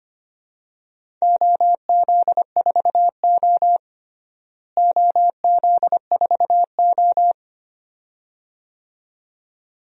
Morse code, heard as a keyed tone, about 25 wpm, 700 Hz. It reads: OZ4O OZ4O
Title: DXCC-100 Call Signs - 4 Characters - Repeated 2x - 25wpm